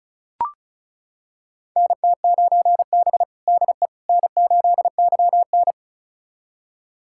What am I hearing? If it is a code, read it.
Morse code, heard as a keyed tone, about 34 wpm, 700 Hz. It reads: NT9B DE N8YN
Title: USA Callsign DE Callsign - 4 Characters - 35wpm